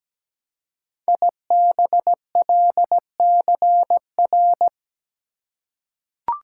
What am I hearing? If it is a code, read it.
Morse code, heard as a keyed tone, about 17 wpm, 700 Hz. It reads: IBLCR
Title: Code Groups - 5 Letter:Numbers - 17wpm